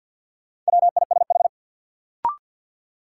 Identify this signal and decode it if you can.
Morse code, heard as a keyed tone, about 53 wpm, 700 Hz. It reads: WISH
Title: Sentences from Top 1600 Words - New Word Drill - 50wpm